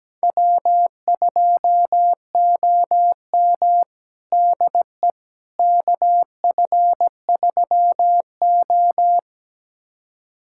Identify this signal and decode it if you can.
Morse code, heard as a keyed tone, about 17 wpm, 700 Hz. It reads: W2OM DE KF3O
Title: USA Callsign DE Callsign - 4 Characters - 17wpm